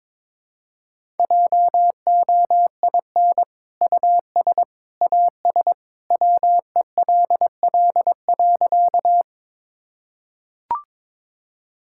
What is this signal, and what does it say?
Morse code, heard as a keyed tone, about 22 wpm, 700 Hz. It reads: JOIN US AS WELL.
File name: practice / 22wpm / Sentences with 2-Word Phrases - 22wpm